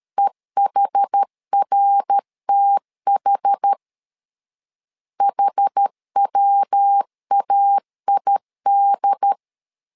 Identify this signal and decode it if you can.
Morse code, heard as a keyed tone, about 13 wpm, 800 Hz. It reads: EHRTH HWAID
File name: LEZIONI / 4 Q 12wpm 1833